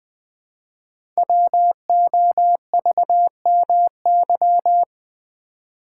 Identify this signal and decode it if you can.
Morse code, heard as a keyed tone, about 20 wpm, 700 Hz. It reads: WOVMY
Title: Code Groups - 5 Letters - 20wpm